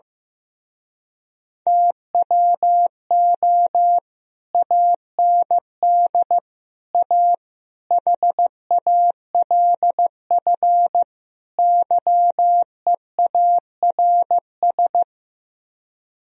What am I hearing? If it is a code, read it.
Morse code, heard as a keyed tone, about 15 wpm, 700 Hz. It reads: TWO AND A HALF YEARS
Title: Speed-Racing - Sets of 5 Words 15wpm